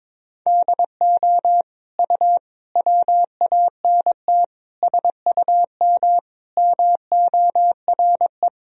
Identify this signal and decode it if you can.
Morse code, heard as a keyed tone, about 22 wpm, 700 Hz. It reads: DO U WANT SUM MORE
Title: Sentences from Top 300 Words and Abbreviations 22wpm